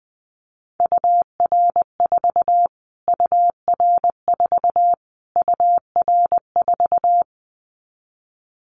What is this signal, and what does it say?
Morse code, heard as a keyed tone, about 20 wpm, 700 Hz. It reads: UR4 UR4 UR4
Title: Callsign Prefixes for Top 25 DXCC - Repeated 3x - 20wpm